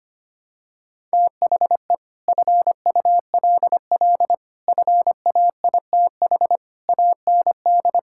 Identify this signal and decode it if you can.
Morse code, heard as a keyed tone, about 25 wpm, 700 Hz. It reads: THE FULL FAITH AND
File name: Sets of 9 Words - 25wpm